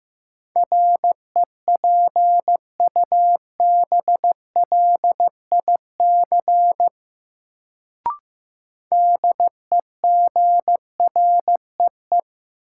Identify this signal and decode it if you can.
Morse code, heard as a keyed tone, about 16 wpm, 700 Hz. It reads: REPUBLIC DEGREE
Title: Sentences from Top 1500 Words - New Word Drill Alt - 15wpm